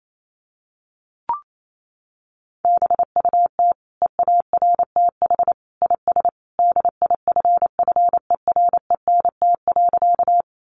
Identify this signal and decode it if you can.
Morse code, heard as a keyed tone, about 28 wpm, 700 Hz. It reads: BUT EARTH IS DIFFERENT.
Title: Sentences from Top 600 Words 28wpm